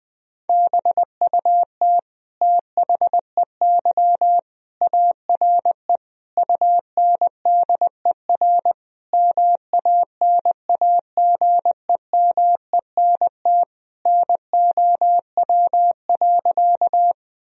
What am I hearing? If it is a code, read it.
Morse code, heard as a keyed tone, about 20 wpm, 700 Hz. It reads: BUT THEY ARE UNDER MANAGEMENT NOW.
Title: Sentences from Top 1000 Words - Review Part 4 - 20wpm